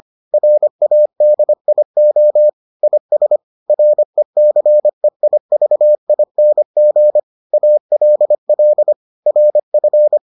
Morse code, heard as a keyed tone, about 25 wpm, 600 Hz. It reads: RADIO IS RECEIVING ALL RF